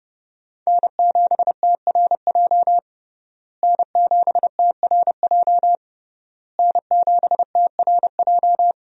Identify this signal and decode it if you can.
Morse code, heard as a keyed tone, about 30 wpm, 700 Hz. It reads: N7TRJ N7TRJ N7TRJ
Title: USA Call Signs - 5 Characters - Repeated 3x - 30wpm